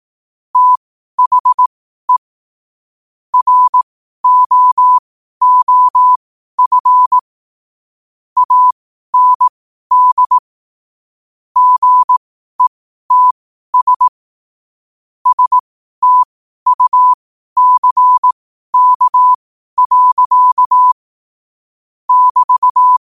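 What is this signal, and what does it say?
Morse code, heard as a keyed tone, about 18 wpm, spaced as for 12 wpm, 1000 Hz. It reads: THE ROOF AND GETS STUCK. <BT>
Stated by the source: Morse code at 12 wpm.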